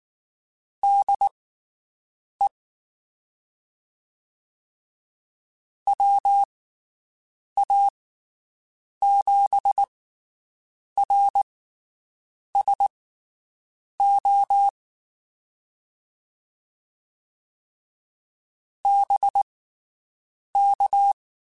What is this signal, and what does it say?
Morse code, heard as a keyed tone, about 19 wpm, spaced as for 7 wpm, 800 Hz. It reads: DE WA7RSO BK
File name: (spaced) / QSOs Spaced-mp3s / WD0BGP 1 @18 sp